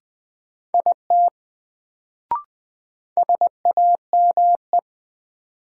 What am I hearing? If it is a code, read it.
Morse code, heard as a keyed tone, about 20 wpm, 700 Hz. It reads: IT SAME
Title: Sets of 2 Words - New Word Drill - 20wpm